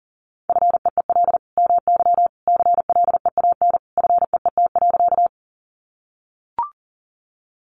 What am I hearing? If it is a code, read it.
Morse code, heard as a keyed tone, about 40 wpm, 700 Hz. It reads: FEEL MY CLEAN FEET.
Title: Sentences from Top 1100 Words 40wpm